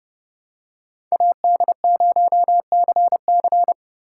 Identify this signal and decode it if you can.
Morse code, heard as a keyed tone, about 29 wpm, 700 Hz. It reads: AD0CC